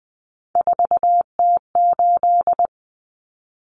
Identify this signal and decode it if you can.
Morse code, heard as a keyed tone, about 20 wpm, 700 Hz. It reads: 4T8
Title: Common Cut Numbers - 3-Digit 20wpm